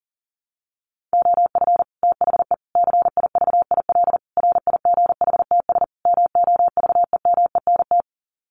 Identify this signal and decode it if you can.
Morse code, heard as a keyed tone, about 40 wpm, 700 Hz. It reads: OF THE CIVIL RIGHTS MOVEMENT
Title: Sets of 5 Words Encore2 40wpm